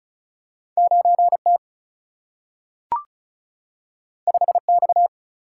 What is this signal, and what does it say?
Morse code, heard as a keyed tone, about 36 wpm, 700 Hz. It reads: 9T 5X
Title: Single-Letter Single-Number 35wpm